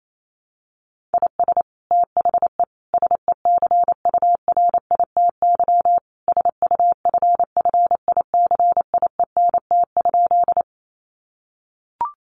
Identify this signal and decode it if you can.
Morse code, heard as a keyed tone, about 28 wpm, 700 Hz. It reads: IS THE SECURITY SUFFICIENT?
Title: Sentences from Top 1400 Words 28wpm